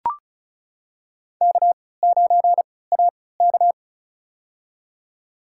Morse code, heard as a keyed tone, about 36 wpm, spaced as for 20 wpm, 700 Hz. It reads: K9AK
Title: USA Call Signs - 3x Character Spacing - 35wpm